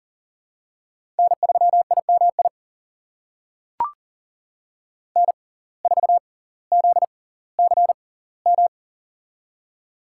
Morse code, heard as a keyed tone, about 40 wpm, spaced as for 20 wpm, 700 Hz. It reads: N3IMI N4ZCM
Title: USA Call Signs - 5 Characters - 6x Character Spacing - 40wpm